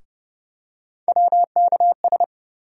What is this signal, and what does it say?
Morse code, heard as a keyed tone, about 31 wpm, 700 Hz. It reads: WKS
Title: USA Call Sign Postfixes - 3 Letters 30wpm